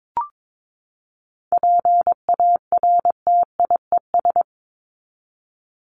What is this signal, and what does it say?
Morse code, heard as a keyed tone, about 22 wpm, 700 Hz. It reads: PARTIES
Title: ES:S Suffix - Top 100 Words with Base Words 22wpm